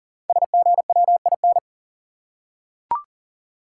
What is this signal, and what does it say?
Morse code, heard as a keyed tone, about 40 wpm, 690 Hz. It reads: SGWIN
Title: Code Groups - 5 Letters - 40wpm